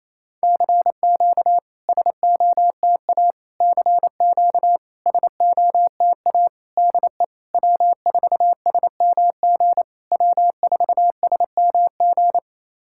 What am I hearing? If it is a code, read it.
Morse code, heard as a keyed tone, about 28 wpm, 700 Hz. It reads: CQ SOTA CQ SOTA DE W4SMG W4SMG
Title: SOTA QSO Practice 28wpm